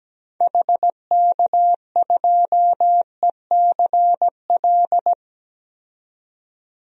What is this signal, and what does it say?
Morse code, heard as a keyed tone, about 17 wpm, 700 Hz. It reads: HK2ECL
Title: DXCC-100 Call Signs - 6 Characters - 17wpm